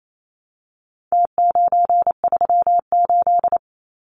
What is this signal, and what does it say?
Morse code, heard as a keyed tone, about 28 wpm, 700 Hz. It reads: T938
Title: Numbers - Contest Style - 4-Digit 28wpm